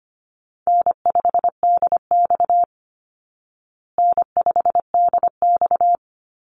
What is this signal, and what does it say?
Morse code, heard as a keyed tone, about 25 wpm, 700 Hz. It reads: N5DX N5DX
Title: Call Signs Repeated 3x - 25wpm